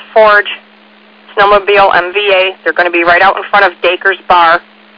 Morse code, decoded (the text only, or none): none